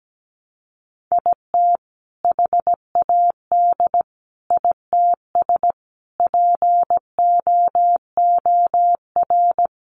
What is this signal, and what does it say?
Morse code, heard as a keyed tone, about 17 wpm, 700 Hz. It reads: IT HAD ITS POOR
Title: Sentences from Top 1000 Words 17wpm